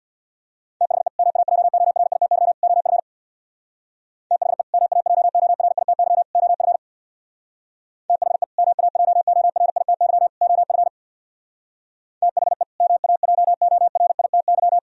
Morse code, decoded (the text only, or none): THE MAJORITY OF THE MAJORITY OF THE MAJORITY OF THE MAJORITY